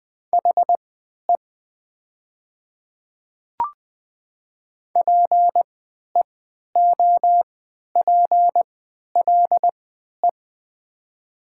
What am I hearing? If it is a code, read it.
Morse code, heard as a keyed tone, about 20 wpm, spaced as for 11 wpm, 700 Hz. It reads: HE PEOPLE
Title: 3x Character Spacing - Top 100 Words - 20wpm